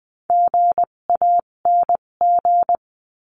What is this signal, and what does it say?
Morse code, heard as a keyed tone, about 20 wpm, spaced as for 17 wpm, 700 Hz. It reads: GANG
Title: 257 Lesson 42b - Words 14-20wpm